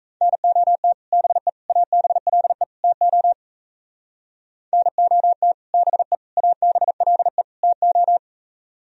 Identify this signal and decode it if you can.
Morse code, heard as a keyed tone, about 35 wpm, 700 Hz. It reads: NOT BE ABLE TO NOT BE ABLE TO